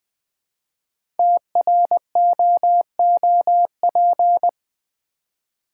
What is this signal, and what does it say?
Morse code, heard as a keyed tone, about 20 wpm, 700 Hz. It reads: TROOP
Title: 5 Letter Words - First 3 Letters Given - 20wpm